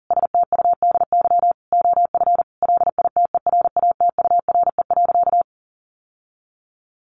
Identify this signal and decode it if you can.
Morse code, heard as a keyed tone, about 40 wpm, 700 Hz. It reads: STUDY OF LITERATURE.
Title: Sentences from Top 900 Words 40wpm